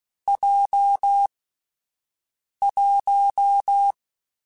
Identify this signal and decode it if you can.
Morse code, heard as a keyed tone, about 16 wpm, 800 Hz. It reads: J 1